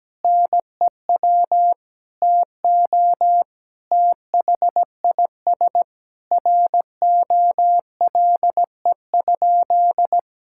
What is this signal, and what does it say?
Morse code, heard as a keyed tone, about 17 wpm, 700 Hz. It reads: NEW TO THIS ROLE?
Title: Sentences from Top 1000 Words - Review Part 6 - 17wpm